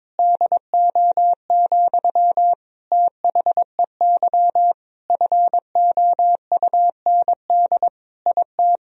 Morse code, heard as a keyed tone, about 21 wpm, 700 Hz. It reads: DO, THEY FOUND IT
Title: Sentences from Top 1000 Words - Review Part 4 - 22wpm